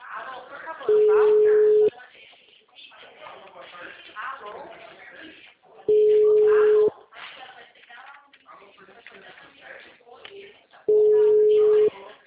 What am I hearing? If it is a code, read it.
Ringback tone (Europe, 425 Hz), with speech.
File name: out- -2001- - - .29507